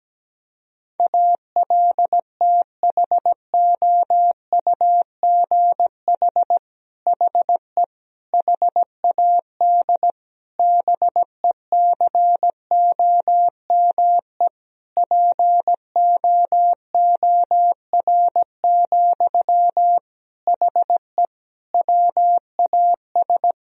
Morse code, heard as a keyed tone, about 17 wpm, 700 Hz. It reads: ALTHOUGH HE HAD BECOME POOR, HE WAS
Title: Sentences from Top 1000 Words - Review Part 3 - 17wpm